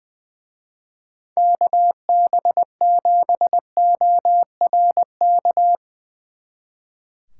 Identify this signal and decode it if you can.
Morse code, heard as a keyed tone, about 21 wpm, 700 Hz. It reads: KB7ORK
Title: Speed Racing - USA 2x3 Call Signs 20wpm